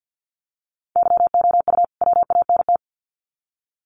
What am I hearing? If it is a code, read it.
Morse code, heard as a keyed tone, about 50 wpm, 700 Hz. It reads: YOU WANT
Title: Sets of 2 Words 50wpm